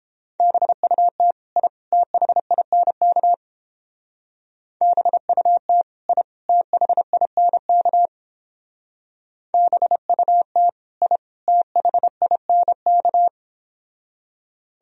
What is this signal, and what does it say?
Morse code, heard as a keyed tone, about 31 wpm, 700 Hz. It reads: BUT I THINK BUT I THINK BUT I THINK